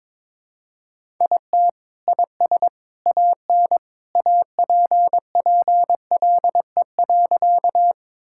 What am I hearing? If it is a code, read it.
Morse code, heard as a keyed tone, about 22 wpm, 700 Hz. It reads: IT IS AN APPLE.
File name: practice / 22wpm / Sentences with 2-Word Phrases - 22wpm